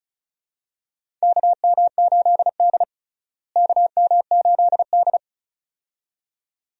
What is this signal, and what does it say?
Morse code, heard as a keyed tone, about 36 wpm, 700 Hz. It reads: KM8D KM8D
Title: USA Call Signs - 4 Characters - Repeated 2x - 35wpm